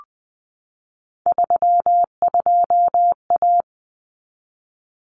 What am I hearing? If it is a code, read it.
Morse code, heard as a keyed tone, about 20 wpm, 700 Hz. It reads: 32A
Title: Common Cut Numbers - 3-Digit 20wpm